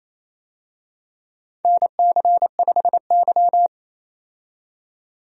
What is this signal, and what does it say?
Morse code, heard as a keyed tone, about 28 wpm, 700 Hz. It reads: NC5Y
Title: USA Call Signs - 4 Characters - 28wpm